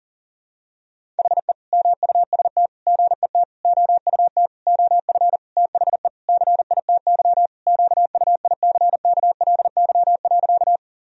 Morse code, heard as a keyed tone, about 40 wpm, 700 Hz. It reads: HE MUST GET OUT OF THE CITY QUICKLY.
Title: Sentences from Top 1000 Words - Review Part 1 - 40wpm